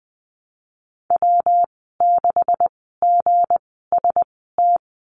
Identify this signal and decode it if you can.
Morse code, heard as a keyed tone, about 20 wpm, spaced as for 15 wpm, 700 Hz. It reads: W6GST
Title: 167 Lesson 27g - Number 7 - Call Signs 20wpm